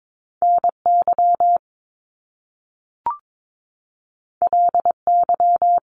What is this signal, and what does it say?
Morse code, heard as a keyed tone, about 22 wpm, 700 Hz. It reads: NY LY
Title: Top Two Letter Combinations in Words 22wpm